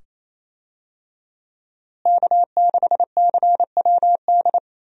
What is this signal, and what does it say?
Morse code, heard as a keyed tone, about 28 wpm, 700 Hz. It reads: K6CWD
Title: USA Call Signs - 5 Characters - Repeated 2x - 28wpm